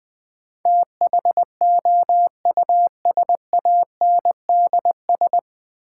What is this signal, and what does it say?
Morse code, heard as a keyed tone, about 20 wpm, 700 Hz. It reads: THOUSANDS
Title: Sets of 4 Words - New Word Drill - 20wpm